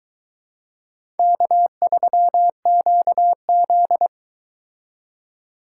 Morse code, heard as a keyed tone, about 22 wpm, 700 Hz. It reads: K3QZ